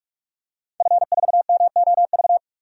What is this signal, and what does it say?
Morse code, heard as a keyed tone, about 46 wpm, 700 Hz. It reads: F4MOV
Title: DXCC-100 Call Signs - 5 Characters - 45wpm